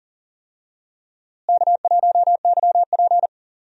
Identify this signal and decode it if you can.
Morse code, heard as a keyed tone, about 40 wpm, 700 Hz. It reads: K1YP
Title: USA Call Signs - 4 Characters - Repeated 3x - 40wpm